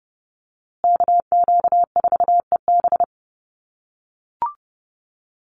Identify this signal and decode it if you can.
Morse code, heard as a keyed tone, about 30 wpm, 700 Hz. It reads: KQ4EB
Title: USA Call Signs Repeated 3x 30wpm